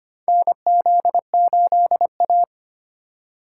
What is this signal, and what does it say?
Morse code, heard as a keyed tone, about 25 wpm, 700 Hz. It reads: NZ8A
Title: USA Call Signs - 4 Characters - 3x Character Spacing - 25wpm